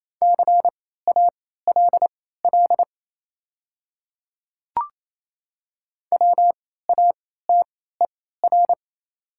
Morse code, heard as a keyed tone, about 27 wpm, spaced as for 16 wpm, 690 Hz. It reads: CALL WATER
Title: 3x Character Spacing - Top 100 Words - 28wpm